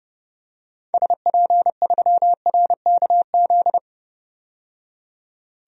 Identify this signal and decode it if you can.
Morse code, heard as a keyed tone, about 30 wpm, 700 Hz. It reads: SP3RKZ
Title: DXCC-100 Call Signs - 6 Characters - 30wpm